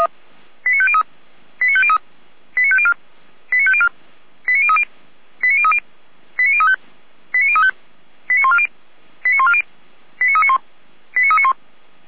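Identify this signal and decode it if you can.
Five-tone selective call (ZVEI), heard as quick runs of five tones, tones of about 70 ms. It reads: 89503 89503 89504 89504 89030 89030 89035 89035 89140 89140 89391 89391
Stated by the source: ZVEI five-tone call received on an SDR.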